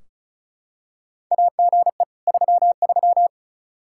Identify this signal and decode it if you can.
Morse code, heard as a keyed tone, about 36 wpm, 700 Hz. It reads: AGE 33
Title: Practice QSO - Exchange 3 - More Info - 35wpm